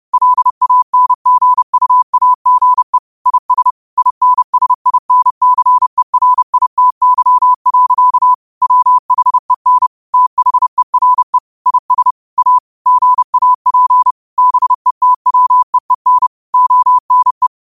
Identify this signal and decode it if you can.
Morse code, heard as a keyed tone, about 30 wpm, 1000 Hz. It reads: LANGUAGE IS INSINCERITY. WHEN THERE IS A GAP BETWEEN ONE